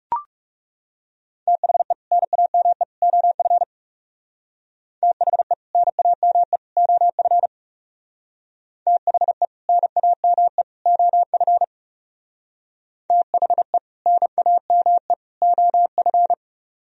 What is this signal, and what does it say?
Morse code, heard as a keyed tone, about 40 wpm, 700 Hz. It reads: THE NAME OF THE NAME OF THE NAME OF THE NAME OF